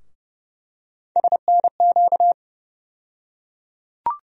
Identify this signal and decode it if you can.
Morse code, heard as a keyed tone, about 30 wpm, 700 Hz. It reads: SNQ
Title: USA Call Sign Postfixes - 3 Letters 30wpm